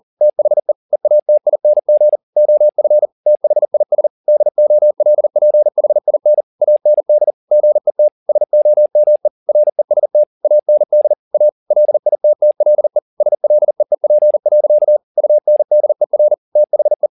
Morse code, heard as a keyed tone, about 40 wpm, 600 Hz. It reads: THE EATING OF THIS DOLPHIN AND GET SOME REST AND A LITTLE SLEEP. UNDER THE
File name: the old man and the sea 2 40wpm